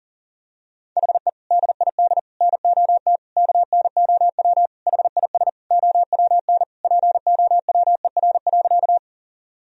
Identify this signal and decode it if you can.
Morse code, heard as a keyed tone, about 40 wpm, 700 Hz. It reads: HE DID NOT KNOW HIS OWN POWER.
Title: Sentences from Top 1000 Words - Review Part 2 - 40wpm